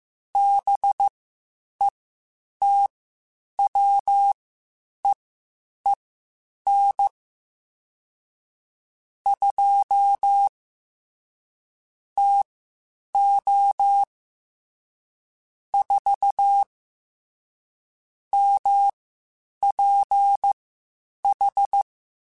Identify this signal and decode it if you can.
Morse code, heard as a keyed tone, about 15 wpm, spaced as for 8 wpm, 800 Hz. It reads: BETWEEN 2 TO 4 MPH
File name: Spaced Characters / Spaced Characters-mp3s / Weather 4 14 sp @13